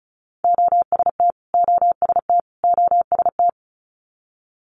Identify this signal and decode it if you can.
Morse code, heard as a keyed tone, about 36 wpm, 700 Hz. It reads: OST OST OST
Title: Multi-letter Phonemes Repeated 3x 35wpm